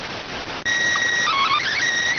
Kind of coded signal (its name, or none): SSTV (slow-scan television)